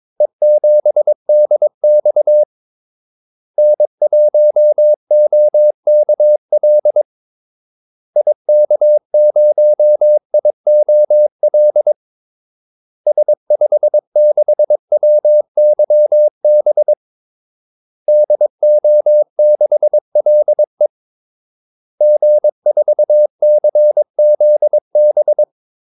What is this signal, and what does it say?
Morse code, heard as a keyed tone, about 22 wpm, 600 Hz. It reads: E7DX N1OKL IK0IOL S56WYB DO6LE G4CZB
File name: Callsigns-22wpm